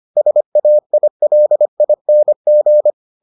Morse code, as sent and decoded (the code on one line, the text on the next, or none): ... .- .. .-.. .. -. --.
SAILING